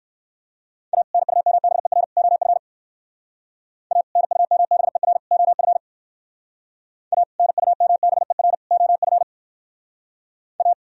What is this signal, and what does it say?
Morse code, heard as a keyed tone, about 56 wpm, 700 Hz. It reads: A NUMBER OF A NUMBER OF A NUMBER OF A
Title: Speed-Racing - Sets of 3 Words 45wpm